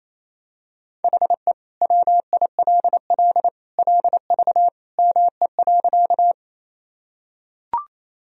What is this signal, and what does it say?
Morse code, heard as a keyed tone, about 28 wpm, 700 Hz. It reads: HE WILL LV ME.
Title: Sentences from Top 400 Words and Abbreviations 28wpm